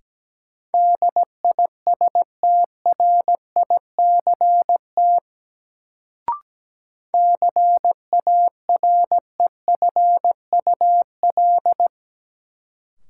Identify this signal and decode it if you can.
Morse code, heard as a keyed tone, about 17 wpm, 700 Hz. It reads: DISTRICT CAREFUL
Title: Sentences from Top 1600 Words - New Word Drill - 17wpm